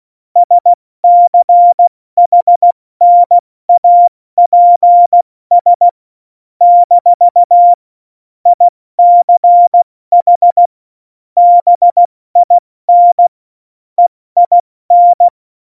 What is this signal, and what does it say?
Morse code, heard as a keyed tone, about 16 wpm, 700 Hz. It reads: SCHNAPS - ICH BIN EIN
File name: Morsekurs / Witze / Witz 03. November 2020, 70 F80